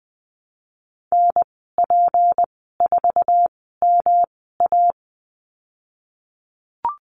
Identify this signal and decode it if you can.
Morse code, heard as a keyed tone, about 20 wpm, spaced as for 14 wpm, 700 Hz. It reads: NP4MA
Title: 154 Lesson 25g - Letter p - Call Signs 20wpm